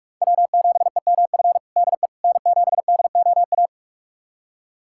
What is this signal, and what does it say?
Morse code, heard as a keyed tone, about 41 wpm, 700 Hz. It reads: W7EMF DE N7DOA
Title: USA Callsign DE Callsign - 5 Characters - 45wpm